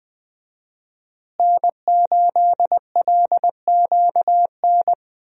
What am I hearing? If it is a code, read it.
Morse code, heard as a keyed tone, about 20 wpm, 700 Hz. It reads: N8LQN
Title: USA Call Signs - 5 Characters - Repeated 2x - 20wpm